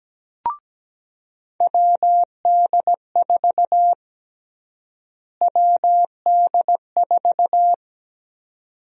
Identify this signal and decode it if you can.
Morse code, heard as a keyed tone, about 17 wpm, 700 Hz. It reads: WD4 WD4
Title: USA Call Sign Prefixes - Repeated 2x - 17wpm